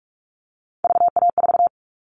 Morse code, heard as a keyed tone, about 44 wpm, 700 Hz. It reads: VA4
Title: All Cut Numbers - 3-Digit 45wpm